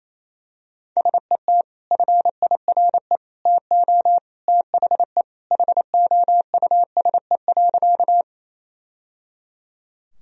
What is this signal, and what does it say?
Morse code, heard as a keyed tone, about 28 wpm, 700 Hz. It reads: SET FIRE TO THE HOUSE.
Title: Sentences from Top 1000 Words - Review Part 1 - 28wpm